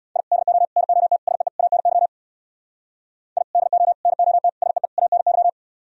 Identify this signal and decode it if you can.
Morse code, heard as a keyed tone, about 64 wpm, 700 Hz. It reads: I DO NOT SEE ANY I DO NOT SEE ANY